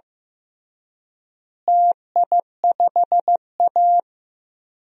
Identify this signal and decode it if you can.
Morse code, heard as a keyed tone, about 15 wpm, 700 Hz. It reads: TI5A
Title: DXCC-100 Call Signs - 4 Characters - Repeated 3x - 15wpm